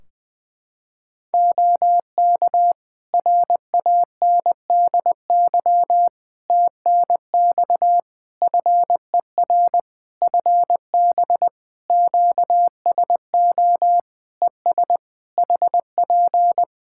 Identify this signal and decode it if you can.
Morse code, heard as a keyed tone, about 20 wpm, 700 Hz. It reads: OK RANDY TNX FER FB QSO ES HP
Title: Practice QSO - Exchange 4 - Ending - 20wpm